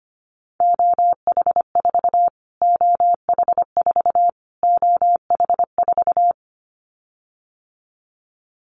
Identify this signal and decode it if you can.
Morse code, heard as a keyed tone, about 25 wpm, 700 Hz. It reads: OH4 OH4 OH4
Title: Callsign Prefixes for Top 25 DXCC - Repeated 3x - 25wpm